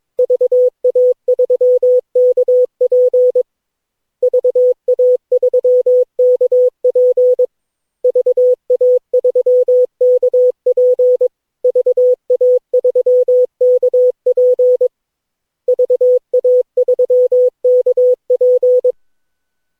Morse code, heard as a keyed tone, about 22 wpm, 500 Hz. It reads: VA3KP VA3KP VA3KP VA3KP VA3KP
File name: cw / Session9 cw training